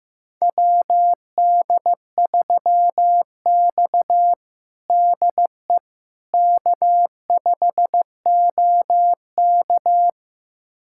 Morse code, heard as a keyed tone, about 15 wpm, 700 Hz. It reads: WD3X DE K5OK
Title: USA Callsign DE Callsign - 4 Characters - 15wpm